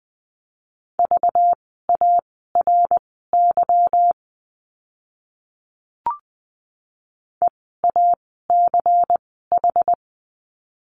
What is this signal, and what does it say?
Morse code, heard as a keyed tone, about 20 wpm, spaced as for 15 wpm, 700 Hz. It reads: VARY EACH
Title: 192 Lesson 31f - Letter v - Words 20wpm